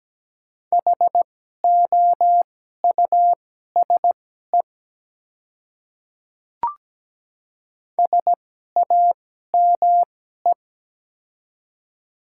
Morse code, heard as a keyed tone, about 17 wpm, spaced as for 12 wpm, 700 Hz. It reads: HOUSE SAME
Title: 2x Character Spacing - Top 200 Words - 17wpm